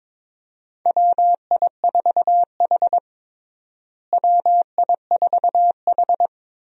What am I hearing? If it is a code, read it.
Morse code, heard as a keyed tone, about 22 wpm, 700 Hz. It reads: WI4H WI4H
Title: USA Call Signs - 4 Characters - Repeated 2x - 22wpm